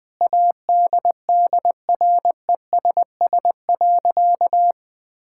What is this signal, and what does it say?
Morse code, heard as a keyed tone, about 20 wpm, 700 Hz. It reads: ADDRESS.
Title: Sentences from Top 1000 Words - Review Part 6 - 20wpm